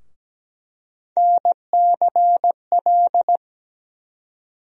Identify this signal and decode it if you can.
Morse code, heard as a keyed tone, about 17 wpm, 700 Hz. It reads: NCL
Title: USA Call Sign Postfixes - 3 Letters 17wpm